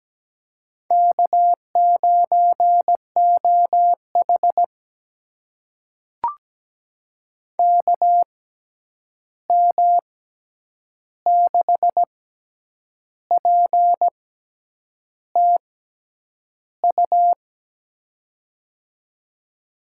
Morse code, heard as a keyed tone, about 17 wpm, spaced as for 8 wpm, 700 Hz. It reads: K9OH KM6PTU
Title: USA Call Signs - 6x Character Spacing - 17wpm